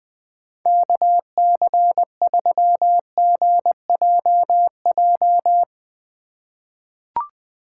Morse code, heard as a keyed tone, about 21 wpm, 700 Hz. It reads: KC3GJJ
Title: USA Call Signs - 6 Characters - 4x Character Spacing - 20wpm